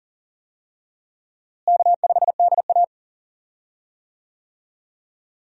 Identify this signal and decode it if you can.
Morse code, heard as a keyed tone, about 40 wpm, 700 Hz. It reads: K5DA
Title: USA Call Signs - 4 Characters - Repeated 2x - 40wpm